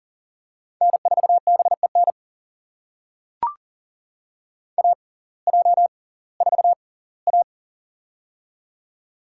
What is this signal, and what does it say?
Morse code, heard as a keyed tone, about 40 wpm, spaced as for 22 wpm, 700 Hz. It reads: N4BEN AJ4A
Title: USA Call Signs - 6x Character Spacing - 40wpm